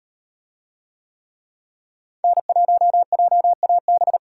Morse code, heard as a keyed tone, about 37 wpm, 700 Hz. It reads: N1JAB